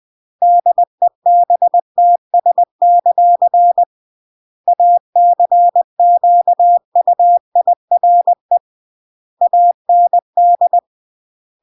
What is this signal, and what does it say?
Morse code, heard as a keyed tone, about 20 wpm, 700 Hz. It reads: DEBTS; ACQUIRE AND
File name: CW / jodorowsky / 82 commandments 20-20